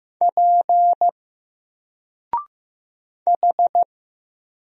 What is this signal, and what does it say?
Morse code, heard as a keyed tone, about 15 wpm, 700 Hz. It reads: P H
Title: Single Letter-Number Alt 15wpm